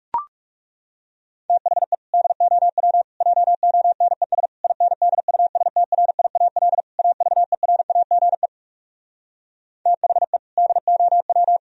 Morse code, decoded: THE DOW JONES INDUSTRIAL AVERAGE THE DOW

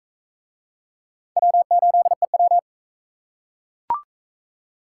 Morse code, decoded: W8EW

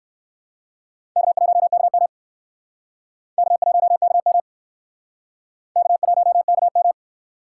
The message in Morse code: K1KM K1KM K1KM

